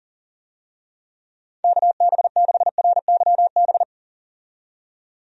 Morse code, heard as a keyed tone, about 40 wpm, 700 Hz. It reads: KB6RYB
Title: USA Call Signs - 6 Characters - 40wpm